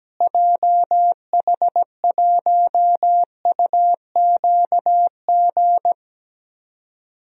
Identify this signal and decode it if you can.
Morse code, heard as a keyed tone, about 17 wpm, 700 Hz. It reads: JH1UQG
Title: DXCC-100 Call Signs - 6 Characters - Repeated 3x - 17wpm